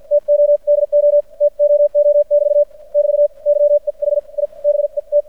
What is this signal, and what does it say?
Morse code, heard as a keyed tone, about 54 wpm, spaced as for 43 wpm, 590 Hz. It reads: TODO TOO <BT> <BT> YEH I BET
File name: archived / CRSnet